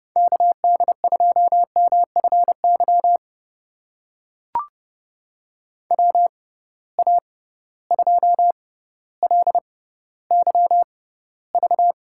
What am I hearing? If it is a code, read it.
Morse code, heard as a keyed tone, about 30 wpm, spaced as for 15 wpm, 700 Hz. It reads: KD2MFY WA2LYV
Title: USA Call Signs - 6 Characters - 6x Character Spacing - 30wpm